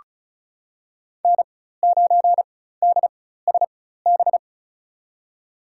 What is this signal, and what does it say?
Morse code, heard as a keyed tone, about 34 wpm, spaced as for 16 wpm, 690 Hz. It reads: N9DSB
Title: USA Call Signs - 4x Character Spacing - 35wpm